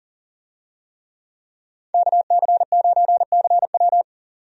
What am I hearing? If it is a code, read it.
Morse code, heard as a keyed tone, about 40 wpm, 700 Hz. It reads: KC9CW
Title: USA Call Signs - 5 Characters - 40wpm